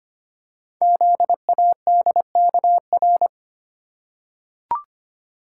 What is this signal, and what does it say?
Morse code, heard as a keyed tone, about 25 wpm, 700 Hz. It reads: ZADKR
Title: Code Groups - 5 Letters - 25wpm